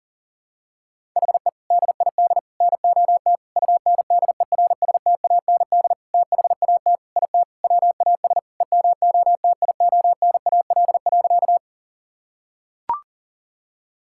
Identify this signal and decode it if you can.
Morse code, heard as a keyed tone, about 40 wpm, 700 Hz. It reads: HE DID NOT UNDERSTAND THAT IT WAS EMOTIONAL.
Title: Sentences from Top 1000 Words - Review Part 5 - 40wpm